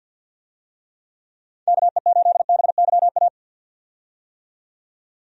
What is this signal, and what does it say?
Morse code, heard as a keyed tone, about 45 wpm, 700 Hz. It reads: KE8BYA
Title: USA Call Signs - 6 Characters - 50wpm